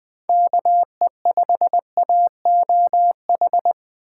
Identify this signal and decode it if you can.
Morse code, heard as a keyed tone, about 21 wpm, 700 Hz. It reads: KE5AOH